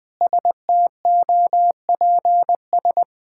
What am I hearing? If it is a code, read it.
Morse code, heard as a keyed tone, about 20 wpm, 700 Hz. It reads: STOPS
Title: Sentences with 2-Word Phrases - 20wpm